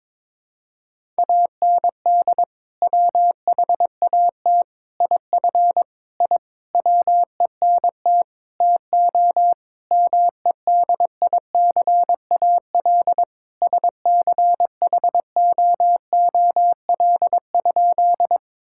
Morse code, heard as a keyed tone, about 22 wpm, 700 Hz. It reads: AND WHAT IF I WENT TO MEDICAL SCHOOL?
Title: Sentences from Top 1000 Words - Review Part 5 - 22wpm